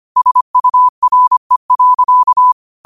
Morse code, heard as a keyed tone, about 25 wpm, 1000 Hz. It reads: SURE.
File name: qotdfiles / 25wpm